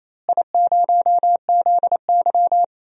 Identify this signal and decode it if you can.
Morse code, heard as a keyed tone, about 28 wpm, 700 Hz. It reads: I0ZY
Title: DXCC-100 Call Signs - 4 Characters - Repeated 3x - 28wpm